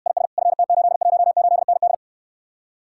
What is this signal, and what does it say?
Morse code, heard as a keyed tone, about 69 wpm, 700 Hz. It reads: IS FT818ND